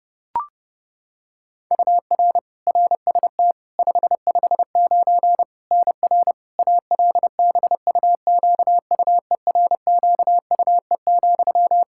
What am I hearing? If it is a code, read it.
Morse code, heard as a keyed tone, about 32 wpm, 700 Hz. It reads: UR RST 559 NR ALBUQUERQUE,
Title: Practice QSO - Exchange 1 - Beginning - 30wpm